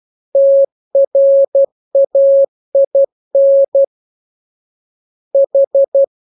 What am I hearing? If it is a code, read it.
Morse code, heard as a keyed tone, about 12 wpm, 550 Hz. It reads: TRAIN H